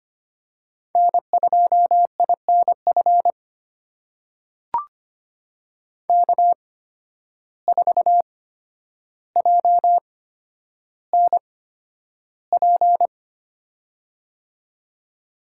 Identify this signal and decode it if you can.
Morse code, heard as a keyed tone, about 25 wpm, spaced as for 11 wpm, 700 Hz. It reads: N2INF K4JNP
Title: USA Call Signs - 5 Characters - 8x Character Spacing - 25wpm